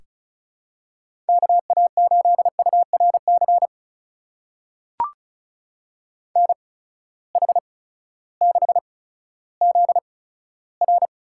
Morse code, heard as a keyed tone, about 36 wpm, spaced as for 16 wpm, 700 Hz. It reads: KA8URC NH6ZR
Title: USA Call Signs - 8x Character Spacing - 35wpm